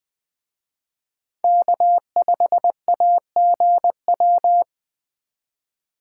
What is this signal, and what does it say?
Morse code, heard as a keyed tone, about 21 wpm, 700 Hz. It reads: K5AGW
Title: Speed Racing - USA Call Signs 17wpm